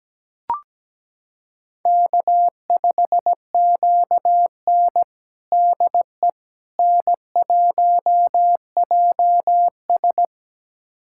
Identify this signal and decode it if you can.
Morse code, heard as a keyed tone, about 17 wpm, 700 Hz. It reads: K5QN DE N1JS
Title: USA Callsign DE Callsign - 4 Characters - 17wpm